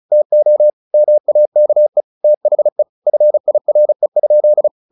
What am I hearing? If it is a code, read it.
Morse code, heard as a keyed tone, about 34 wpm, 600 Hz. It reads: TO MAKE THE FIRE?
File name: the old man and the sea / the old man and the sea 1 35wpm